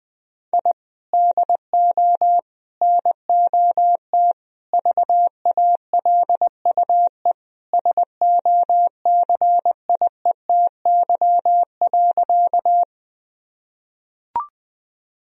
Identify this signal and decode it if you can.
Morse code, heard as a keyed tone, about 20 wpm, 700 Hz. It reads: I DO NOT VALUE SOCIETY.
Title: Sentences from Top 1000 Words - Review Part 4 - 20wpm